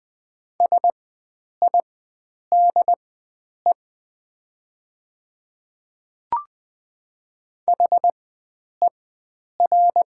Morse code, heard as a keyed tone, about 20 wpm, spaced as for 9 wpm, 700 Hz. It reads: SIDE HER
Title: 4x Character Spacing - Top 100 Words - 20wpm